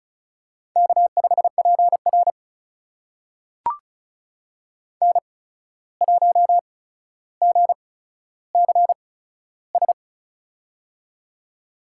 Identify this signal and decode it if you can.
Morse code, heard as a keyed tone, about 36 wpm, spaced as for 14 wpm, 700 Hz. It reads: K5PR N1GCS
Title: USA Call Signs - 8x Character Spacing - 35wpm